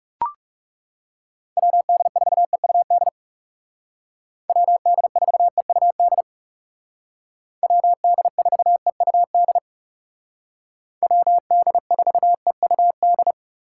Morse code, heard as a keyed tone, about 39 wpm, 700 Hz. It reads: WD4EUD WD4EUD WD4EUD WD4EUD